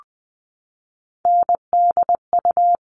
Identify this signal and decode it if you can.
Morse code, heard as a keyed tone, about 20 wpm, 700 Hz. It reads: NDU
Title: All Cut Numbers - 3-Digit 20wpm